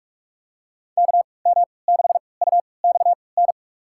Morse code, heard as a keyed tone, about 46 wpm, spaced as for 26 wpm, 700 Hz. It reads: KM6UXN
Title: USA Call Signs - 6 Characters - 3x Character Spacing - 45wpm